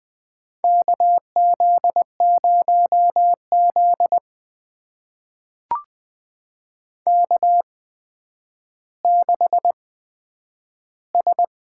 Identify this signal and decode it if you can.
Morse code, heard as a keyed tone, about 20 wpm, spaced as for 10 wpm, 700 Hz. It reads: KZ0Z K6S
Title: USA Call Signs - 4 Characters - 8x Character Spacing - 20wpm